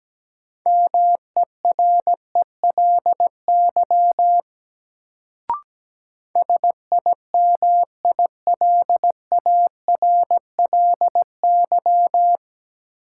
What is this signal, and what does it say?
Morse code, heard as a keyed tone, about 17 wpm, 700 Hz. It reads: MERELY SIMILARLY
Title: Sentences from Top 1400 Words - New Word Drill Alt - 17wpm